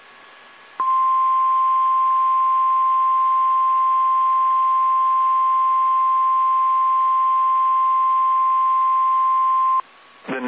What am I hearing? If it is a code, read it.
Emergency Alert System (SAME) tones (the 1050 Hz attention tone). It is heard from 0.7 to 9.9 s.
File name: eas / KSDZKDJL / KSDZKDJL